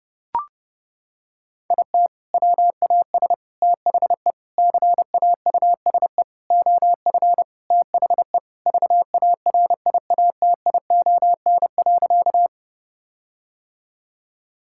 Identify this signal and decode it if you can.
Morse code, heard as a keyed tone, about 30 wpm, 700 Hz. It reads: IT WAS THE CAUSE OF THE VARIATION.
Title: Sentences from Top 1600 Words 30wpm